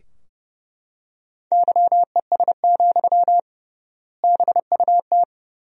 Morse code, heard as a keyed tone, about 30 wpm, 700 Hz. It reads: YES, BUT
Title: 3x Word-Spacing - Sentences from Top 500 Words - 30wpm